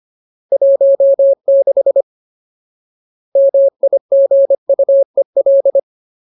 Morse code, heard as a keyed tone, about 25 wpm, 550 Hz. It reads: 16 MIGUEL
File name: Ultimate Word List / 25WPM Ultimate